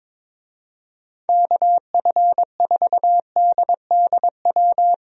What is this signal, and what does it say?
Morse code, heard as a keyed tone, about 22 wpm, 700 Hz. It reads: KF4DDW
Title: USA Call Signs - 6 Characters - 22wpm